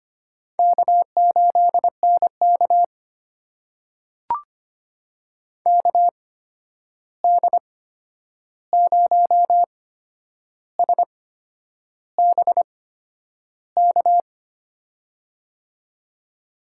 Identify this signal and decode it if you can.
Morse code, heard as a keyed tone, about 25 wpm, spaced as for 9 wpm, 700 Hz. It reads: K8NK KD0SBK
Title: USA Call Signs - 8x Character Spacing - 25wpm